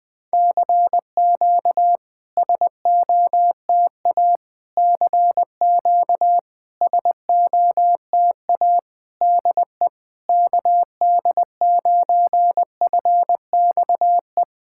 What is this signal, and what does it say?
Morse code, heard as a keyed tone, about 20 wpm, 700 Hz. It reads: CQ SOTA CQ SOTA DE KD9FXE
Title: SOTA QSO Practice 20wpm